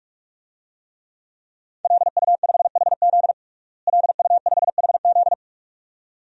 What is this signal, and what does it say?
Morse code, heard as a keyed tone, about 45 wpm, 700 Hz. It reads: LU5HZ LU5HZ
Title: DXCC-100 Call Signs - 5 Characters - Repeated 2x - 45wpm